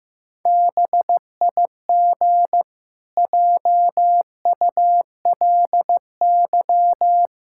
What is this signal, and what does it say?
Morse code, heard as a keyed tone, about 14 wpm, 700 Hz. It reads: BIG JULY
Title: Pangram Challenge v2 - 15wpm